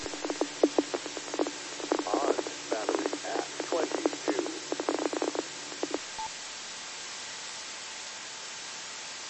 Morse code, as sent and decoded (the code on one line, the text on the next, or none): none